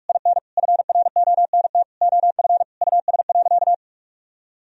Morse code, heard as a keyed tone, about 46 wpm, 700 Hz. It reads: IN FRONT OF US.
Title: Sentences from Top 1000 Words - Review Part 2 - 45wpm